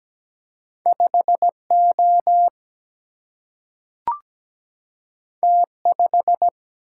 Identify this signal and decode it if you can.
Morse code, heard as a keyed tone, about 17 wpm, 700 Hz. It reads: 5O T5
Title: Single-Letter Single-Number 17wpm